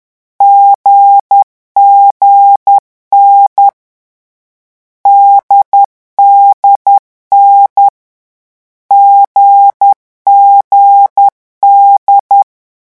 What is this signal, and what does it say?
Morse code, heard as a keyed tone, about 10 wpm, 800 Hz. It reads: GGN DDN GGD
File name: morse mp3 / lecon4-vitesse 10